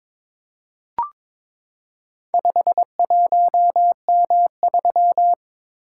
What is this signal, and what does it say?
Morse code, heard as a keyed tone, about 22 wpm, 700 Hz. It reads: 51M3
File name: practice / 22wpm / Four Letter-Number Combinations - 22wpm